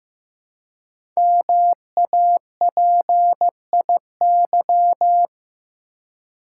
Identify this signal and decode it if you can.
Morse code, heard as a keyed tone, about 15 wpm, 700 Hz. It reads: MAPIY
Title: Code Groups - 5 Letters - 15wpm